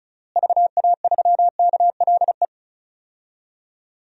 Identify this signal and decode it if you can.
Morse code, heard as a keyed tone, about 36 wpm, 700 Hz. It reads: VA3KLE
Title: DXCC-100 Call Signs - 6 Characters - Repeated 3x - 35wpm